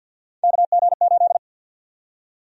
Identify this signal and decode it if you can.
Morse code, heard as a keyed tone, about 52 wpm, 700 Hz. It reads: KG8
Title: USA Call Sign Prefixes - Repeated 2x - 50wpm